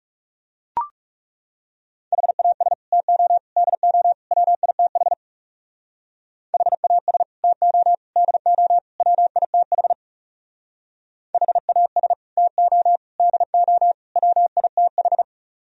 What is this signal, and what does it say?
Morse code, heard as a keyed tone, about 35 wpm, 700 Hz. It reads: HAS TO DO WITH HAS TO DO WITH HAS TO DO WITH
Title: Speed-Racing - Sets of 4 Words 30wpm